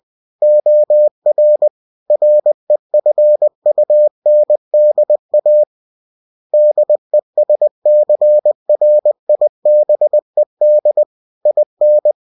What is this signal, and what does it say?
Morse code, heard as a keyed tone, about 20 wpm, 600 Hz. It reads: OR REFUNDA DESCRIBED IN